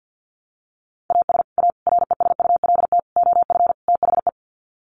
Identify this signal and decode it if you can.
Morse code, heard as a keyed tone, about 48 wpm, 700 Hz. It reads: AS A RESULT OF THE
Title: Sets of 5 Words 50wpm